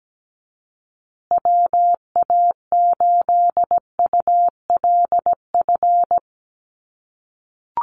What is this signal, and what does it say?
Morse code, heard as a keyed tone, about 17 wpm, 700 Hz. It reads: WA8ULF
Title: USA Call Signs Repeated 2x 17wpm